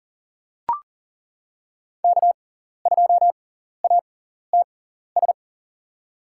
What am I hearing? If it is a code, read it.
Morse code, heard as a keyed tone, about 40 wpm, spaced as for 14 wpm, 700 Hz. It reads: K2ATS
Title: USA Call Signs - 5 Characters - 6x Character Spacing - 40wpm